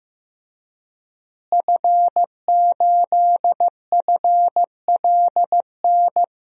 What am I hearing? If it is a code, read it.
Morse code, heard as a keyed tone, about 15 wpm, 700 Hz. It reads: F8FLN
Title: DXCC-100 Call Signs - 5 Characters - Repeated 2x - 15wpm